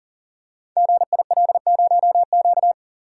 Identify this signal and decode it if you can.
Morse code, heard as a keyed tone, about 42 wpm, 700 Hz. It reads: GIL0Q
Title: Code Groups - 5 Letter:Numbers - 40wpm